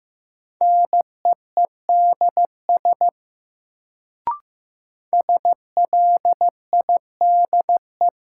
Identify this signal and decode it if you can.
Morse code, heard as a keyed tone, about 15 wpm, 700 Hz. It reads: NEEDS SLIDE
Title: Sentences from Top 1600 Words - New Word Drill Alt - 15wpm